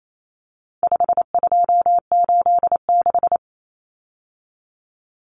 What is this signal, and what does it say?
Morse code, heard as a keyed tone, about 28 wpm, 700 Hz. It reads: 5286
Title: Numbers - 4-Digit 28wpm